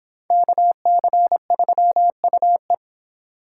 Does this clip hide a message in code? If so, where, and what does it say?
Yes: Morse code from 0.3 to 2.7 s, heard as a keyed tone, about 23 wpm, 700 Hz. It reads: KC3UE